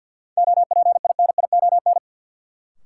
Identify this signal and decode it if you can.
Morse code, heard as a keyed tone, about 53 wpm, 700 Hz. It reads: OPINION
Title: Sentences from Top 1200 Words - New Word Drill Alt - 50wpm